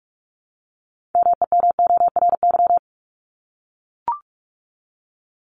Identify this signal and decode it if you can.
Morse code, heard as a keyed tone, about 44 wpm, 700 Hz. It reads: MEMORY
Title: Top 1251-1500 Words 45wpm